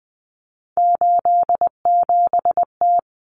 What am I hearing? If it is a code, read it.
Morse code, heard as a keyed tone, about 20 wpm, 700 Hz. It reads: 87T
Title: Common Cut Numbers - 3-Digit 20wpm